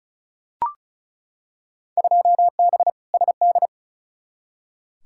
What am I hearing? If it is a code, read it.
Morse code, heard as a keyed tone, about 36 wpm, 700 Hz. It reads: 2B SD
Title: Field Day Exchanges Encore - 35wpm